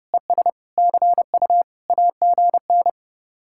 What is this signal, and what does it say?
Morse code, heard as a keyed tone, about 30 wpm, 700 Hz. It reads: ES CU AGN